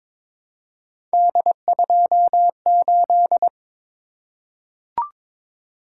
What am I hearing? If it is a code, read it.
Morse code, heard as a keyed tone, about 22 wpm, 700 Hz. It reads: D28
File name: practice / 22wpm / Three Letter-Number Combinations - 22wpm